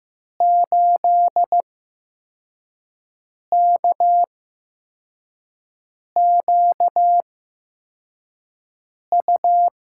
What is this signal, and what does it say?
Morse code, heard as a keyed tone, about 15 wpm, spaced as for 4 wpm, 700 Hz. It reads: 8KQU
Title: USA Call Signs - 5 Characters - 8x Character Spacing - 15wpm